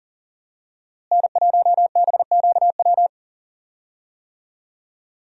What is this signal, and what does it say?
Morse code, heard as a keyed tone, about 37 wpm, 700 Hz. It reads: N1BQW
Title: USA Call Signs - 5 Characters - Repeated 2x - 40wpm